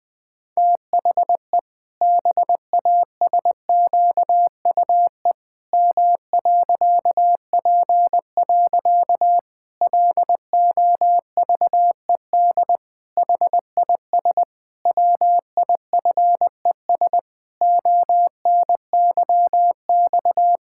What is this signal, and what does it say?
Morse code, heard as a keyed tone, about 20 wpm, 700 Hz. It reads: THE BASQUE M.P. LOVED HIS WIFES ONYX
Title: Pangram Challenge v2 - 20wpm